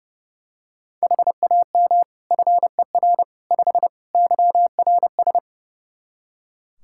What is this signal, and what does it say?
Morse code, heard as a keyed tone, about 30 wpm, 700 Hz. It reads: HAM FER 5 YRS
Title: K1USN SST QSO Practice 30wpm